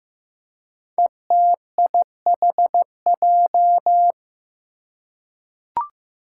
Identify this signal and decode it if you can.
Morse code, heard as a keyed tone, about 15 wpm, 700 Hz. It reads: ETIHJ
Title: Five Challenging Letter-Number Combinations 15wpm